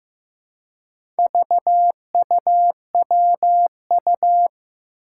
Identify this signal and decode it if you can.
Morse code, heard as a keyed tone, about 15 wpm, 700 Hz. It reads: VUWU
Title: Four Challenging Letter-Number Combinations 15wpm